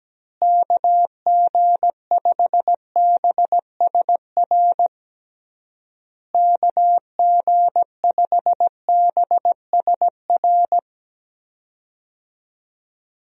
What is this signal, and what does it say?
Morse code, heard as a keyed tone, about 17 wpm, 700 Hz. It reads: KG5BSR KG5BSR
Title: USA Call Signs - 6 Characters - Repeated 3x - 17wpm